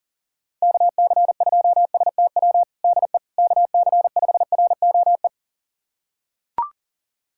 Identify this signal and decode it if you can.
Morse code, heard as a keyed tone, about 40 wpm, 700 Hz. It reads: KC2STW DE KC5ROE
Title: USA Callsign DE Callsign - 6 Characters - 40wpm